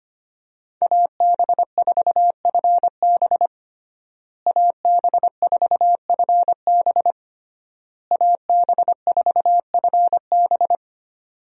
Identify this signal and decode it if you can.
Morse code, heard as a keyed tone, about 25 wpm, 700 Hz. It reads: AB4FB AB4FB AB4FB
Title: USA Call Signs - 5 Characters - Repeated 3x - 25wpm